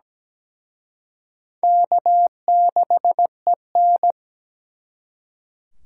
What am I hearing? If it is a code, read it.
Morse code, heard as a keyed tone, about 20 wpm, 700 Hz. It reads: K6EN
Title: Speed Racing - USA 1x2 Call Signs 17wpm